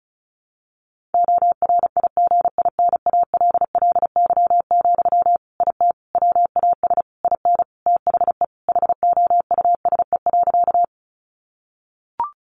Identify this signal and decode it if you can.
Morse code, heard as a keyed tone, about 34 wpm, 700 Hz. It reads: ORIGINALLY, IT WAS IN THE HOUSE.
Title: Sentences from Top 1500 Words 35wpm